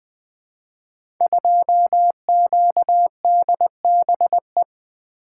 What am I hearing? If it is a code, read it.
Morse code, heard as a keyed tone, about 20 wpm, 700 Hz. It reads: 2QDBE
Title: Code Groups - 5 Letter:Numbers - 20wpm